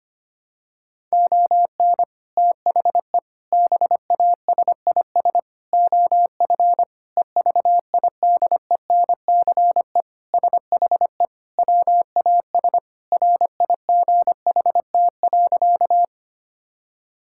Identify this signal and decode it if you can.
Morse code, heard as a keyed tone, about 25 wpm, 700 Hz. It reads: ON THE BASIS OF EVIDENCE SHE WAS RIGHT.
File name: practice / 25wpm / Sentences with 4-Word Phrases - 25wpm